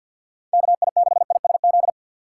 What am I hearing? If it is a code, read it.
Morse code, heard as a keyed tone, about 52 wpm, 700 Hz. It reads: KI6ISZ
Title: USA Call Signs - 3x Character Spacing - 50wpm